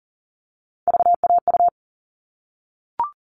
Morse code, heard as a keyed tone, about 40 wpm, 700 Hz. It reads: VAU
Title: All Cut Numbers - 3-Digit 40wpm